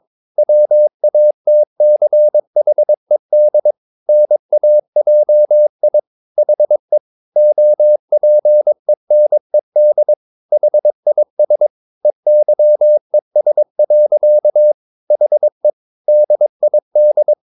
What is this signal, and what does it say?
Morse code, heard as a keyed tone, about 22 wpm, 600 Hz. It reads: WATCHED NAJI HE OPENED HIS EYES. HE DID
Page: Romanen Never af Ken Follet på engelsk.